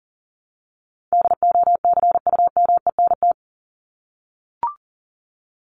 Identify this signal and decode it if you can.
Morse code, heard as a keyed tone, about 40 wpm, 700 Hz. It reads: DOCUMENT
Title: Top 1751-2000 Words 40wpm